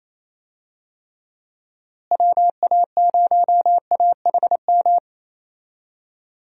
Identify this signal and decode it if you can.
Morse code, heard as a keyed tone, about 27 wpm, 700 Hz. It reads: WA0AHM
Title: USA Call Signs - 6 Characters - Repeated 3x - 28wpm